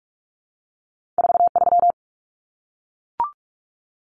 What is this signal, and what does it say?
Morse code, heard as a keyed tone, about 46 wpm, 700 Hz. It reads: 43
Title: Common Cut Numbers - 2-Digit 45wpm